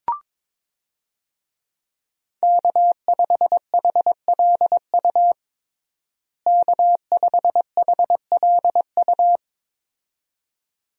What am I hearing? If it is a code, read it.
Morse code, heard as a keyed tone, about 22 wpm, 700 Hz. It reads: K5HLU K5HLU
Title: USA Call Signs - 5 Characters - Repeated 2x - 22wpm